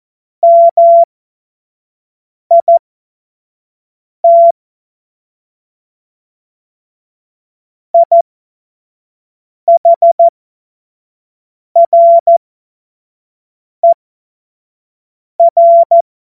Morse code, heard as a keyed tone, about 14 wpm, spaced as for 5 wpm, 680 Hz. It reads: MIT IHRER
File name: Morsekurs / Witze / Witz 15. Februar 2022 25 F70